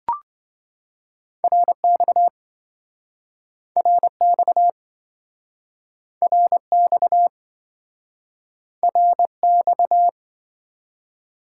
Morse code, heard as a keyed tone, about 27 wpm, 700 Hz. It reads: RX RX RX RX